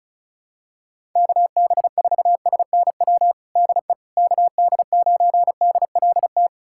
Morse code, heard as a keyed tone, about 36 wpm, 700 Hz. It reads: KB4SNW DE KD9DLT
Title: USA Callsign DE Callsign 35wpm